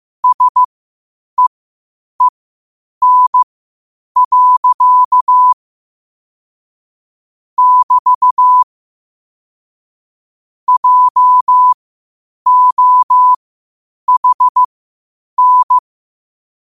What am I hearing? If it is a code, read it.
Morse code, heard as a keyed tone, about 15 wpm, spaced as for 8 wpm, 1000 Hz. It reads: SEEN. <BT> JOHN
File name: qotdfiles / 8wpm